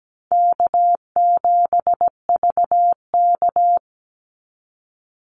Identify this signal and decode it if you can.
Morse code, heard as a keyed tone, about 17 wpm, 700 Hz. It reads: K7VK
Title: NA SOTA Chaser Callsigns 17wpm